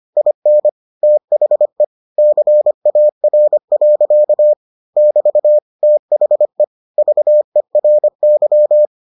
Morse code, heard as a keyed tone, about 25 wpm, 600 Hz. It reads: IN THE CAR. <BT> THE VERY